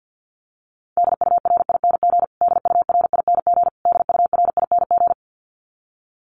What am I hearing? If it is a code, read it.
Morse code, heard as a keyed tone, about 50 wpm, 700 Hz. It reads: DURING DURING DURING
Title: Top 500 Words Repeated 50wpm